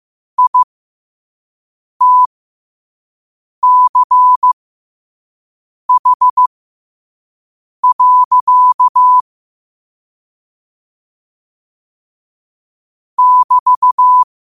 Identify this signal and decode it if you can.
Morse code, heard as a keyed tone, about 15 wpm, spaced as for 5 wpm, 1000 Hz. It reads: ITCH. <BT>